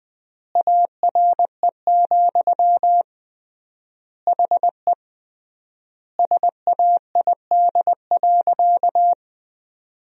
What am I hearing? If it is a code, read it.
Morse code, heard as a keyed tone, about 20 wpm, 700 Hz. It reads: ARE, HE SAID.
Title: 3x Word-Spacing - Sentences from Top 100 Words - 20wpm